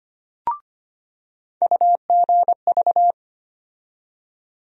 Morse code, heard as a keyed tone, about 25 wpm, 700 Hz. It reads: UGV
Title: USA Call Sign Postfixes - 3 Letters 25wpm